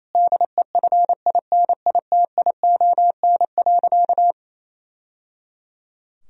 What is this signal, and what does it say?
Morse code, heard as a keyed tone, about 28 wpm, 700 Hz. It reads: DEFINITION.
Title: Sentences from Top 1000 Words - Review Part 4 - 28wpm